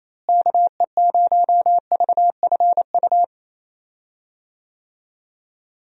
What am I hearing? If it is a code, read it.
Morse code, heard as a keyed tone, about 28 wpm, 700 Hz. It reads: KE0VFU
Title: USA Call Signs - 6 Characters - Repeated 2x - 28wpm